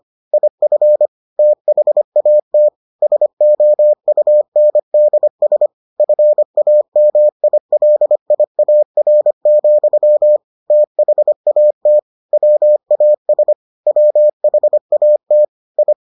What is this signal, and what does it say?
Morse code, heard as a keyed tone, about 25 wpm, 600 Hz. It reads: IF THAT SOUNDS FAMILIAR, THAT WAS WHAT I